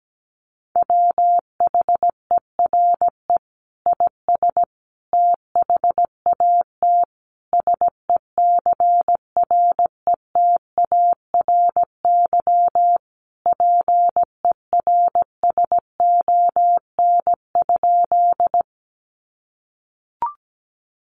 Morse code, heard as a keyed tone, about 17 wpm, 700 Hz. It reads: WHERE IS THAT SECRETARY PERSON?
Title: Sentences from Top 1300 Words 17wpm